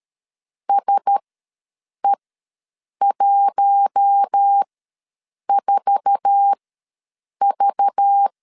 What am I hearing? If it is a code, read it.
Morse code, heard as a keyed tone, about 13 wpm, spaced as for 7 wpm, 800 Hz. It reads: SE14V
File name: LEZIONI / 9 2867 ws 12wpm 3431